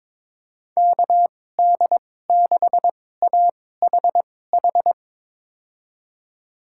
Morse code, heard as a keyed tone, about 22 wpm, spaced as for 16 wpm, 700 Hz. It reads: KD6AHH
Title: USA Call Signs - 6 Characters - 2x Character Spacing - 22wpm